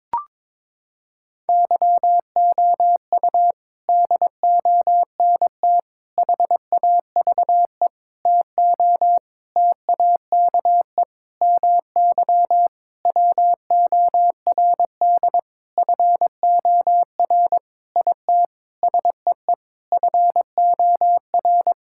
Morse code, heard as a keyed tone, about 22 wpm, 700 Hz. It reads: YOU DONT HAVE TO TAKE MY WORD FOR IT SEE FOR
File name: practice / 22wpm / Sentences with 8-Word Phrases - 22wpm